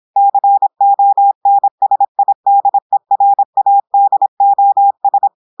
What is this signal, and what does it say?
Morse code, heard as a keyed tone, about 26 wpm, 800 Hz. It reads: CONSIDERADOS
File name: portugais texte en clair 2 vitesse 26 mots minutes